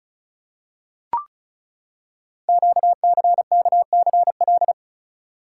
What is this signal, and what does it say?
Morse code, heard as a keyed tone, about 36 wpm, 700 Hz. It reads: QCKCL
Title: Code Groups - 5 Letters - 35wpm